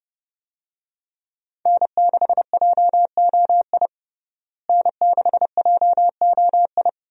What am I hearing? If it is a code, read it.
Morse code, heard as a keyed tone, about 30 wpm, 700 Hz. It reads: N6JOI N6JOI
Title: USA Call Signs - 5 Characters - Repeated 2x - 30wpm